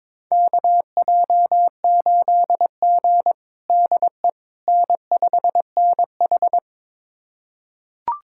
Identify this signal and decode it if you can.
Morse code, heard as a keyed tone, about 22 wpm, 700 Hz. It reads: KJ8G DE N5NH
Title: USA Callsign DE Callsign - 4 Characters - 22wpm